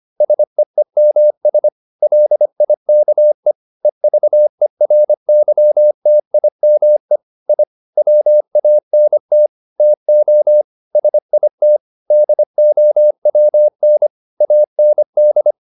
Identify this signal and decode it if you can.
Morse code, heard as a keyed tone, about 25 wpm, 600 Hz. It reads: SEEMS LIKE EVERYTIME I WANT TO SIT DOWN AND